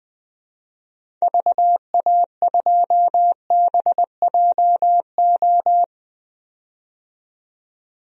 Morse code, heard as a keyed tone, about 20 wpm, 700 Hz. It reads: VA2BJO
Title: DXCC-100 Call Signs - 6 Characters - 20wpm